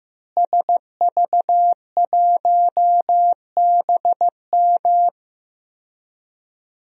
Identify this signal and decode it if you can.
Morse code, heard as a keyed tone, about 15 wpm, 700 Hz. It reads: SV1BM
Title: DXCC-100 Call Signs - 5 Characters - Repeated 3x - 15wpm